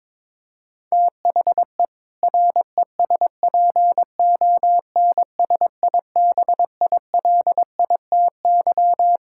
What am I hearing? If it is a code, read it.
Morse code, heard as a keyed tone, about 22 wpm, 700 Hz. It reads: THE RESPONSIBILITY
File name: practice / 22wpm / Sentences with 2-Word Phrases - 22wpm